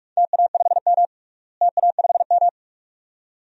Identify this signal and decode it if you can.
Morse code, heard as a keyed tone, about 46 wpm, 700 Hz. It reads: TA5M TA5M
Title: DXCC-100 Call Signs - 4 Characters - Repeated 2x - 45wpm